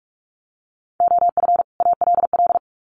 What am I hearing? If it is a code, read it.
Morse code, heard as a keyed tone, about 44 wpm, 700 Hz. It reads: OF ALL
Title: Sets of 2 Words 45wpm